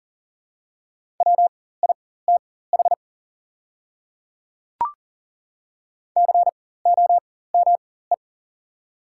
Morse code, heard as a keyed tone, about 40 wpm, spaced as for 19 wpm, 690 Hz. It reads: WITH COME
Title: 4x Character Spacing - Top 100 Words - 40wpm